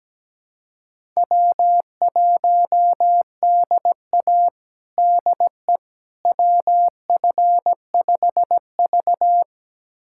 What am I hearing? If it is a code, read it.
Morse code, heard as a keyed tone, about 17 wpm, 700 Hz. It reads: W1DA DE WF5V
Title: USA Callsign DE Callsign - 4 Characters - 17wpm